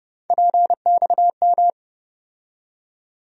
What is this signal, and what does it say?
Morse code, heard as a keyed tone, about 30 wpm, 700 Hz. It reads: PXM
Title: USA Call Sign Postfixes - 3 Letters 30wpm